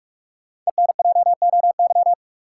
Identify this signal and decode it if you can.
Morse code, heard as a keyed tone, about 47 wpm, 700 Hz. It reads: ENJOY
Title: Sentences from Top 1300 Words - New Word Drill Alt - 45wpm